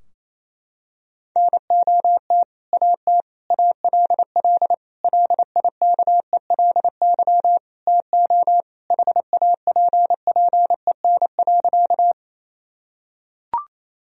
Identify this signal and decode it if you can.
Morse code, heard as a keyed tone, about 28 wpm, 700 Hz. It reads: NOT AT ALL LIKELY TO HAPPEN.
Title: Sentences from Top 1000 Words - Review Part 4 - 28wpm